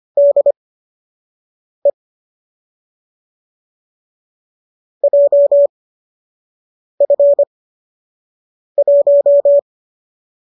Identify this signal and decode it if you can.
Morse code, heard as a keyed tone, about 25 wpm, spaced as for 6 wpm, 570 Hz. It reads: DE JF1